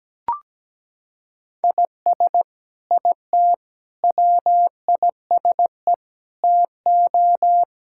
Morse code, heard as a keyed tone, about 17 wpm, 700 Hz. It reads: IS IT WISE TO
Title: Sentences from Top 1600 Words 17wpm